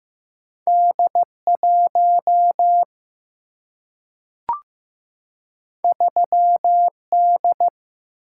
Morse code, heard as a keyed tone, about 15 wpm, 700 Hz. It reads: D 1 3 D
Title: Single-Letter Single-Number 15wpm